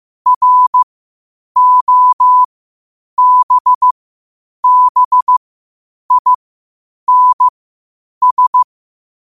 Morse code, heard as a keyed tone, about 15 wpm, spaced as for 8 wpm, 1000 Hz. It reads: ROBBINS
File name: qotdfiles / 8wpm